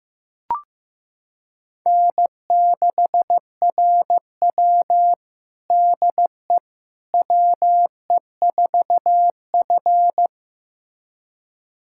Morse code, heard as a keyed tone, about 15 wpm, 700 Hz. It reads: N6RW DE WE4F
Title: USA Callsign DE Callsign - 4 Characters - 15wpm